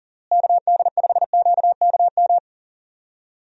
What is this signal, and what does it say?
Morse code, heard as a keyed tone, about 40 wpm, 700 Hz. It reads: KD5QKM
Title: USA Call Signs - 6 Characters - Repeated 3x - 40wpm